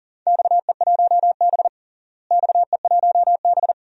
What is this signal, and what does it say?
Morse code, heard as a keyed tone, about 37 wpm, 700 Hz. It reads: XE1B XE1B
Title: DXCC-100 Call Signs - 4 Characters - Repeated 2x - 40wpm